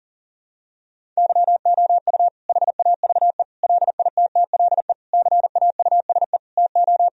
Morse code, heard as a keyed tone, about 40 wpm, 700 Hz. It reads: YOU HAVE LITTLE CAUSE TO
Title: Sentences from Top 1000 Words - Review Part 1 - 40wpm